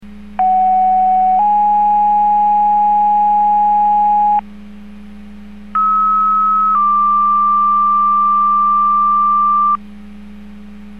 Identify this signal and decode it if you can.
Two-tone page, heard as a one-second tone, then a three-second tone, about 750 then 830 Hz, about 1290 then 1180 Hz. It is heard from 0.4 to 9.8 s.